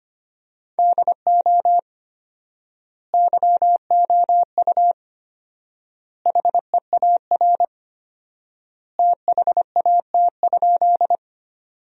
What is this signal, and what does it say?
Morse code, heard as a keyed tone, about 25 wpm, 700 Hz. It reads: DO YOU HEAR THAT?
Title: 4x Word-Spacing - Sentences from Top 300 Words - 25wpm